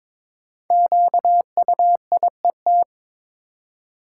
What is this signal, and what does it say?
Morse code, heard as a keyed tone, about 22 wpm, 700 Hz. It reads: QUIET
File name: practice / 22wpm / 5 Letter Words - First 3 Letters Given - 22wpm